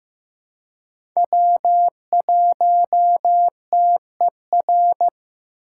Morse code, heard as a keyed tone, about 15 wpm, 700 Hz. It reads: W1TER
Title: K1USN SST QSO Practice - 15wpm